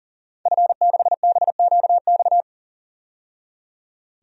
Morse code, heard as a keyed tone, about 40 wpm, 700 Hz. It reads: F6BQX
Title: DXCC-100 Call Signs - 5 Characters - 40wpm